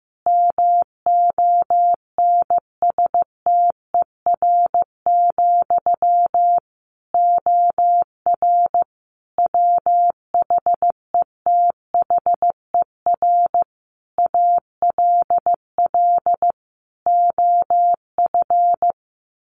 Morse code, heard as a keyed tone, about 15 wpm, 700 Hz. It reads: MONSTER, OR WHETHER ALL OF
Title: A Sound Of Thunder 15wpm